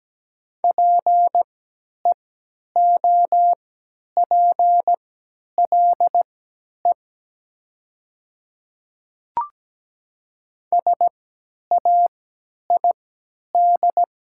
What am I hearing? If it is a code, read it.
Morse code, heard as a keyed tone, about 17 wpm, spaced as for 10 wpm, 700 Hz. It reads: PEOPLE SAID
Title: 3x Character Spacing - Top 100 Words - 17wpm